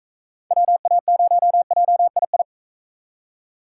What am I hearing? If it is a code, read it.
Morse code, heard as a keyed tone, about 38 wpm, 700 Hz. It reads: WA0JII